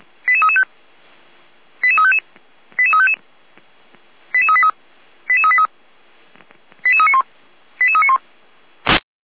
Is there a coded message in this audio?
Yes: five-tone selective call (ZVEI) from 0.3 to 8.2 s, heard as quick runs of five tones, tones of about 70 ms. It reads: 88335 88355 88355 88383 88383 88391 88391